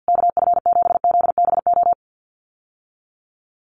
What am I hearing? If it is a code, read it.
Morse code, heard as a keyed tone, about 50 wpm, 700 Hz. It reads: KF7ZBO
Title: USA Call Signs Repeated 3x 50wpm